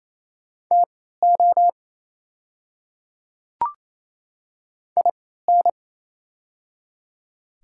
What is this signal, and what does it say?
Morse code, heard as a keyed tone, about 28 wpm, spaced as for 16 wpm, 700 Hz. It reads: TO IN
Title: 3x Character Spacing - Top 100 Words - 28wpm